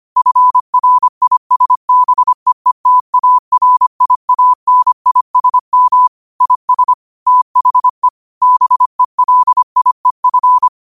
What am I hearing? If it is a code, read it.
Morse code, heard as a keyed tone, about 25 wpm, 1000 Hz. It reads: FRISBEETARIANISM IS THE BELIEF
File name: qotdfiles / 25wpm